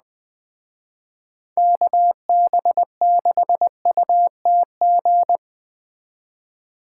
Morse code, heard as a keyed tone, about 20 wpm, 700 Hz. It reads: KB6UTG
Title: Speed Racing - USA 2x3 Call Signs 17wpm